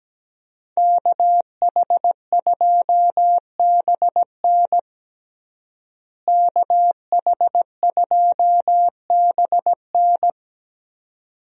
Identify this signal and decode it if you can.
Morse code, heard as a keyed tone, about 17 wpm, 700 Hz. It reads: KH2BN KH2BN
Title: USA Call Signs - 5 Characters - Repeated 2x - 17wpm